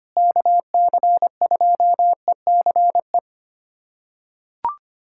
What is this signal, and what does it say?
Morse code, heard as a keyed tone, about 25 wpm, 700 Hz. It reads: KC2ECE
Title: USA Callsign DE Callsign 25wpm